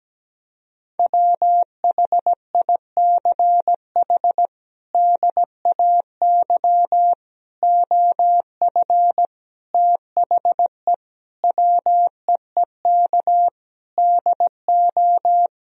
Morse code, heard as a keyed tone, about 17 wpm, 700 Hz. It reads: WHICH DAY OF THE WEEK DO
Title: Sentences from Top 1000 Words - Review Part 3 - 17wpm